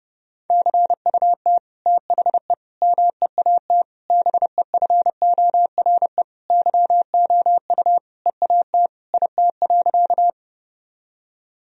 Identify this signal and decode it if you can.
Morse code, heard as a keyed tone, about 32 wpm, 700 Hz. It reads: CUT THE MEAT BEFORE YOU EAT IT.
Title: Sentences from Top 1000 Words - Review Part 3 - 30wpm